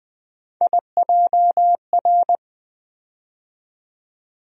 Morse code, heard as a keyed tone, about 20 wpm, 700 Hz. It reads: IJR
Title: USA Call Sign Postfixes - 3 Letters 20wpm